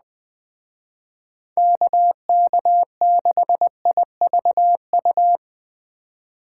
Morse code, heard as a keyed tone, about 20 wpm, 700 Hz. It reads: KK6IVU
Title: Speed Racing - USA 2x3 Call Signs 17wpm